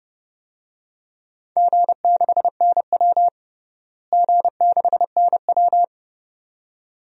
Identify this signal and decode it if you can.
Morse code, heard as a keyed tone, about 30 wpm, 700 Hz. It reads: G6NW G6NW
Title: DXCC-100 Call Signs - 4 Characters - Repeated 2x - 30wpm